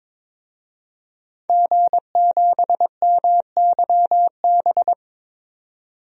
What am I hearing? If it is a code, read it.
Morse code, heard as a keyed tone, about 22 wpm, 700 Hz. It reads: G7MYB
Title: DXCC-100 Call Signs - 5 Characters - 22wpm